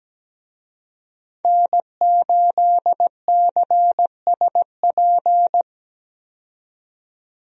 Morse code, heard as a keyed tone, about 17 wpm, 700 Hz. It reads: N8CSP
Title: USA Call Signs - 5 Characters - 17wpm